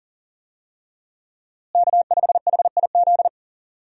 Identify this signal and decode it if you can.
Morse code, heard as a keyed tone, about 39 wpm, 700 Hz. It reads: K5HIZ
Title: USA Call Signs - 5 Characters - Repeated 2x - 40wpm